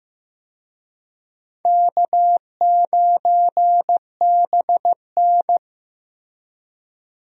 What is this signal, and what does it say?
Morse code, heard as a keyed tone, about 16 wpm, 700 Hz. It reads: K9BN
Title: USA Call Signs - 4 Characters - Repeated 2x - 15wpm